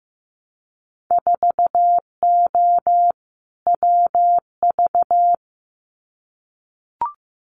Morse code, heard as a keyed tone, about 15 wpm, 700 Hz. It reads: 4O WV
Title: Winter Field Day Exchanges Encore 15wpm